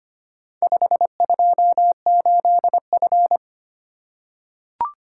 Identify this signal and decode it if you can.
Morse code, heard as a keyed tone, about 25 wpm, 700 Hz. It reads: 528F
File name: practice / 25wpm / Four Letter-Number Combinations - 25wpm